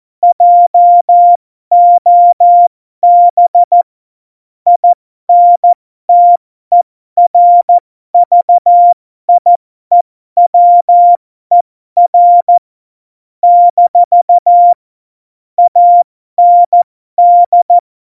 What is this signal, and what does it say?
Morse code, heard as a keyed tone, about 14 wpm, spaced as for 12 wpm, 700 Hz. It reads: JOB INTERVIEWER - AND
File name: Morsekurs / Witze